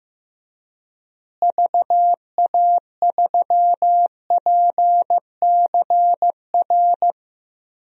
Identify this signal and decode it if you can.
Morse code, heard as a keyed tone, about 16 wpm, 700 Hz. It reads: VA3PCR
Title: DXCC-100 Call Signs - 6 Characters - Repeated 3x - 15wpm